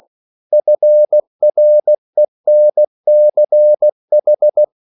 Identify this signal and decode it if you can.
Morse code, heard as a keyed tone, about 16 wpm, 600 Hz. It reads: FRENCH